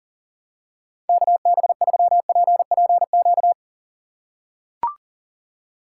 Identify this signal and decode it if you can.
Morse code, heard as a keyed tone, about 40 wpm, 700 Hz. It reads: KB3PPQ
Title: USA Call Signs - 2x Character Spacing - 40wpm